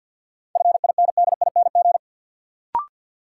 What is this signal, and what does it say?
Morse code, heard as a keyed tone, about 47 wpm, 700 Hz. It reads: FINDING
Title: Sentences from Top 1100 Words - New Word Drill Alt - 50wpm